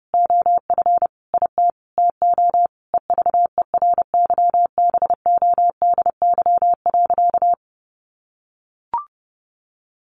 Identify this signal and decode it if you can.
Morse code, heard as a keyed tone, about 30 wpm, 700 Hz. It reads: OF IT TO EVERYBODY.
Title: Sentences from Top 1400 Words 30wpm